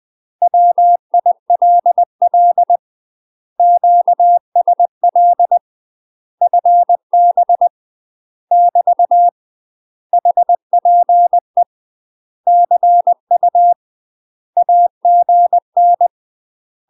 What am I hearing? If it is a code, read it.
Morse code, heard as a keyed tone, about 20 wpm, 700 Hz. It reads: WILL QSL FB <BT> HPE CU AGN
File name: UK MASTER QSOs TRAINING/20WPM